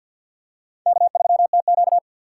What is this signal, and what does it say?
Morse code, heard as a keyed tone, about 46 wpm, 700 Hz. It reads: K3TQ
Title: USA Call Signs - 4 Characters - 2x Character Spacing - 50wpm